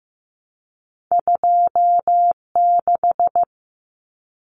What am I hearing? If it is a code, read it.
Morse code, heard as a keyed tone, about 15 wpm, 700 Hz. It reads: 26
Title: Numbers - 2-Digit 15wpm